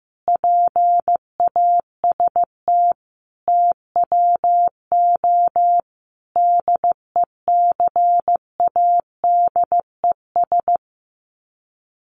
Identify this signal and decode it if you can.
Morse code, heard as a keyed tone, about 14 wpm, 700 Hz. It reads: PAST TWO DECADES
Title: Sets of 5 Words Encore2 15wpm